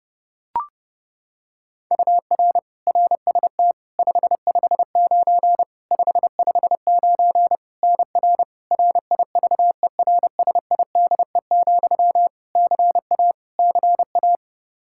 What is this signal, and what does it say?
Morse code, heard as a keyed tone, about 30 wpm, 700 Hz. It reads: UR RST 559 559 NR RIVERSIDE, CA CA
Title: Practice QSO - Exchange 1 - Beginning - 30wpm